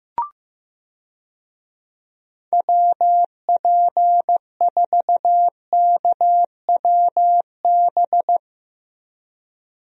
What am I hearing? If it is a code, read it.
Morse code, heard as a keyed tone, about 15 wpm, 700 Hz. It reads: WP4KWB
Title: USA Call Signs - 6 Characters - Repeated 3x - 15wpm